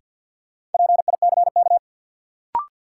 Morse code, heard as a keyed tone, about 52 wpm, 700 Hz. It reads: PICK
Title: Sentences from Top 1000 Words - New Word Drill - 50wpm